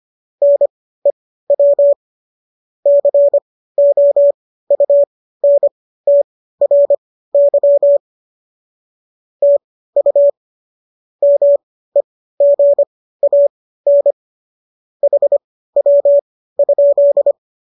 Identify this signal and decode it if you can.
Morse code, heard as a keyed tone, about 25 wpm, spaced as for 15 wpm, 570 Hz. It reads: NEW COUNTRY TU MEGAN HW?